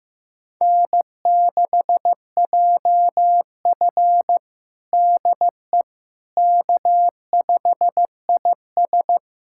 Morse code, heard as a keyed tone, about 15 wpm, 700 Hz. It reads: N6JF DE K5IS
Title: USA Callsign DE Callsign - 4 Characters - 15wpm